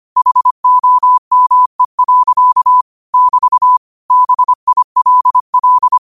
Morse code, heard as a keyed tone, about 25 wpm, 1000 Hz. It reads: HOME. <BT> BILL
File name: qotdfiles / 25wpm